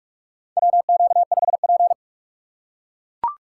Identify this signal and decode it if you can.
Morse code, heard as a keyed tone, about 42 wpm, 700 Hz. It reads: WQ5P
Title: USA Call Signs - 4 Characters - 2x Character Spacing - 45wpm